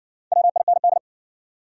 Morse code, heard as a keyed tone, about 57 wpm, 700 Hz. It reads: WIND